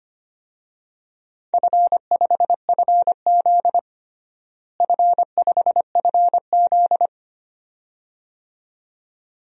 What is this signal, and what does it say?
Morse code, heard as a keyed tone, about 25 wpm, 700 Hz. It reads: F5FZ F5FZ
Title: DXCC-100 Call Signs - 4 Characters - Repeated 2x - 25wpm